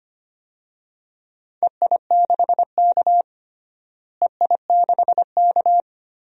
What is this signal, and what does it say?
Morse code, heard as a keyed tone, about 25 wpm, 700 Hz. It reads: EI6K EI6K
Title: DXCC-100 Call Signs - 4 Characters - Repeated 2x - 25wpm